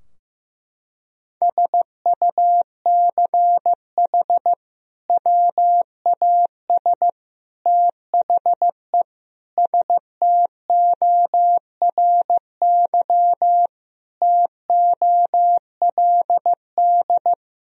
Morse code, heard as a keyed tone, about 15 wpm, 700 Hz. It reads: SUCH WAS THE STORY TOLD
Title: Sentences from Top 1000 Words - Review Part 3 - 15wpm